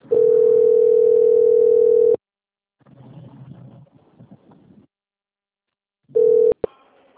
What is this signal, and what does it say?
Ringback tone (North America), 440 + 480 Hz.
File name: out- - - - - .45847